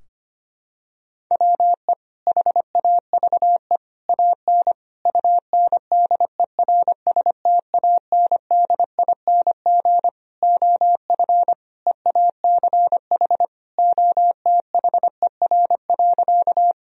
Morse code, heard as a keyed tone, about 25 wpm, 700 Hz. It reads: WE HAVE AN UNDERSTANDING OF EACH OTHER.
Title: Sentences from Top 1000 Words - Review Part 3 - 25wpm